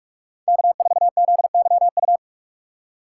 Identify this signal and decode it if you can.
Morse code, heard as a keyed tone, about 43 wpm, 700 Hz. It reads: K4ZYU
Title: USA Call Signs - 5 Characters - 4x Character Spacing - 45wpm